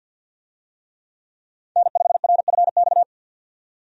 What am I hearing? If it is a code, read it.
Morse code, heard as a keyed tone, about 50 wpm, 700 Hz. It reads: N5RFX
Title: USA Call Signs - 5 Characters - 50wpm